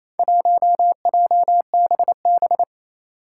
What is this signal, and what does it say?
Morse code, heard as a keyed tone, about 27 wpm, 700 Hz. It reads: 1JBB
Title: Four Challenging Letter-Number Combinations 28wpm